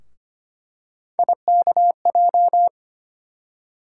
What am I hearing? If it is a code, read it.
Morse code, heard as a keyed tone, about 25 wpm, 700 Hz. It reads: IKJ
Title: USA Call Sign Postfixes - 3 Letters 25wpm